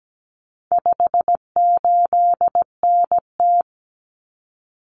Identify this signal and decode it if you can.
Morse code, heard as a keyed tone, about 17 wpm, 700 Hz. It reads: 58NT
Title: Common Cut Numbers - 4-Digit 17wpm